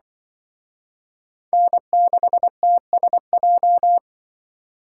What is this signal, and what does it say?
Morse code, heard as a keyed tone, about 25 wpm, 700 Hz. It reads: N6TSJ
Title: Speed Racing - USA 1x3 Call Signs 20wpm